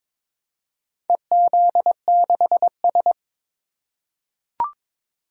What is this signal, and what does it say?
Morse code, heard as a keyed tone, about 22 wpm, 700 Hz. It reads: EZ6S
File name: practice / 22wpm / Four Letter-Number Combinations - 22wpm